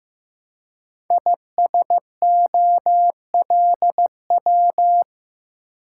Morse code, heard as a keyed tone, about 15 wpm, 700 Hz. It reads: ISOLW
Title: Code Groups - 5 Letter:Numbers - 15wpm